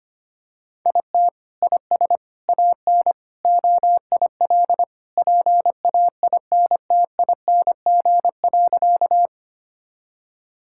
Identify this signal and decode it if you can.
Morse code, heard as a keyed tone, about 25 wpm, 700 Hz. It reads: IT IS AN OIL PAINTING.
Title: Sentences from Top 1000 Words - Review Part 6 - 25wpm